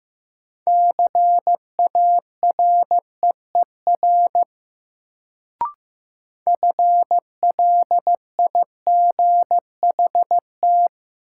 Morse code, heard as a keyed tone, about 15 wpm, 700 Hz. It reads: CAREER FLIGHT
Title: Sentences from Top 1000 Words - New Word Drill - 15wpm